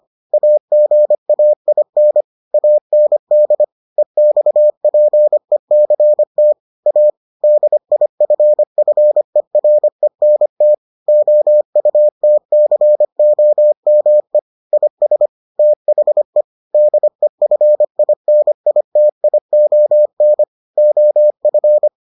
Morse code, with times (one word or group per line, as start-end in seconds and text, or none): 0.3-2.2 AGAIN
2.5-3.6 AND
4.0-6.5 EXPECT
6.9-7.1 A
7.4-10.7 DIFFERENT
11.1-14.4 OUTCOME
14.7-15.3 IS
15.6-16.4 THE
16.7-20.4 DEFINITION
20.8-21.9 OF